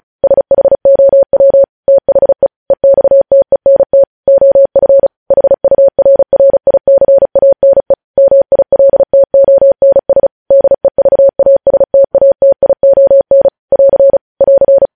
Morse code, heard as a keyed tone, about 36 wpm, 550 Hz. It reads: SHOW THE EXTENT OF HURRICANE MILTONS DEVASTATION <AR> <AR>